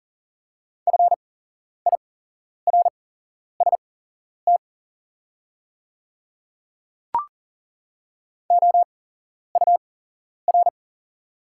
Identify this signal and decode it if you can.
Morse code, heard as a keyed tone, about 40 wpm, spaced as for 11 wpm, 700 Hz. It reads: FIRST OUR
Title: 8x Character Spacing - Top 200 Words - 40wpm